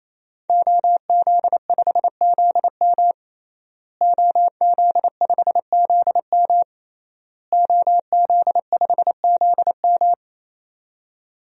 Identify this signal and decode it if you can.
Morse code, heard as a keyed tone, about 27 wpm, 700 Hz. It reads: OZ5ZM OZ5ZM OZ5ZM
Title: DXCC-100 Call Signs - 5 Characters - Repeated 3x - 28wpm